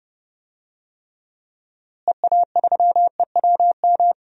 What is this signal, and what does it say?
Morse code, heard as a keyed tone, about 30 wpm, 700 Hz. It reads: EA3EWM
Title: DXCC-100 Call Signs - 6 Characters - Repeated 3x - 30wpm